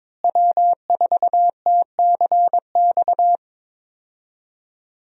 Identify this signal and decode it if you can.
Morse code, heard as a keyed tone, about 22 wpm, 700 Hz. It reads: W4TCX
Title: USA Call Signs - 5 Characters - Repeated 2x - 22wpm